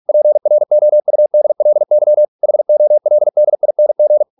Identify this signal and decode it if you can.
Morse code, heard as a keyed tone, about 46 wpm, 600 Hz. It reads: PROUDLY HOLDING